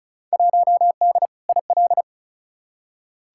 Morse code, heard as a keyed tone, about 34 wpm, 700 Hz. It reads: 1D IL
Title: Field Day Exchanges - 35wpm